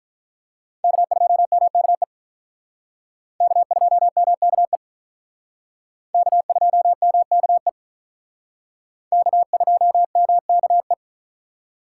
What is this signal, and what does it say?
Morse code, heard as a keyed tone, about 43 wpm, 700 Hz. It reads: K2MKE K2MKE K2MKE K2MKE
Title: Speed Racing - USA 1x3 Call Signs 35wpm